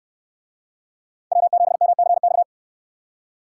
Morse code, heard as a keyed tone, about 64 wpm, 700 Hz. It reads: W7MCX